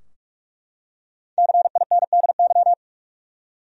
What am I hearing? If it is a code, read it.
Morse code, heard as a keyed tone, about 42 wpm, 700 Hz. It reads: CINDY
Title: USA Common Names - 45wpm